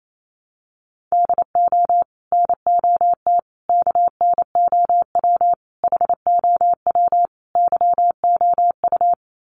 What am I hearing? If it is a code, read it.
Morse code, heard as a keyed tone, about 28 wpm, 700 Hz. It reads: DO NOT KNOW HOW YOU
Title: Sets of 5 Words Encore 28wpm